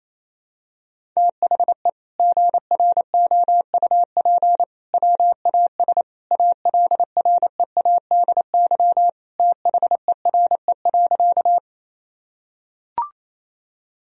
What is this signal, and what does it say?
Morse code, heard as a keyed tone, about 28 wpm, 700 Hz. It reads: THE GROUP WAS ALREADY THERE.
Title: Sentences from Top 1000 Words - Review Part 1 - 28wpm